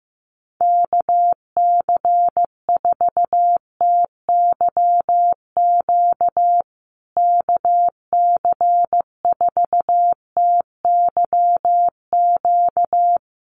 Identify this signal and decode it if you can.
Morse code, heard as a keyed tone, about 15 wpm, 700 Hz. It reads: KC4TYQ KC4TYQ
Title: USA Call Signs Repeated 3x 15wpm